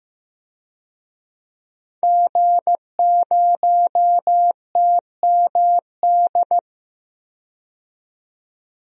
Morse code, heard as a keyed tone, about 15 wpm, 700 Hz. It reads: G0TMD
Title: DXCC-100 Call Signs - 5 Characters - 15wpm